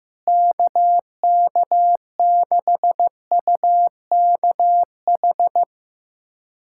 Morse code, heard as a keyed tone, about 15 wpm, 700 Hz. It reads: KK6UKH
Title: USA Callsign DE Callsign - 6 Characters - 15wpm